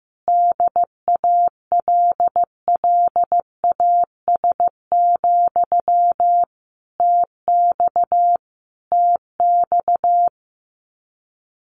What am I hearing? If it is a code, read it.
Morse code, heard as a keyed tone, about 15 wpm, 700 Hz. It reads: DALLAS, TX TX
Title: Common QSO Phrases Encore 15wpm